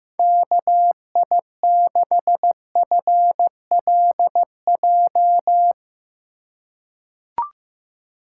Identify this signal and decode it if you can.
Morse code, heard as a keyed tone, about 15 wpm, 700 Hz. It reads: KI6FLJ
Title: USA Call Signs - 6 Characters - 3x Character Spacing - 15wpm